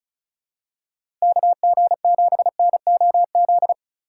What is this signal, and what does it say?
Morse code, heard as a keyed tone, about 34 wpm, 700 Hz. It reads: KG7NOZ
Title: USA Call Signs - 6 Characters - 3x Character Spacing - 35wpm